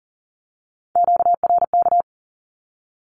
Morse code, heard as a keyed tone, about 38 wpm, 700 Hz. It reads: QRK
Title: Q-Codes with Spoken Meaning - 40wpm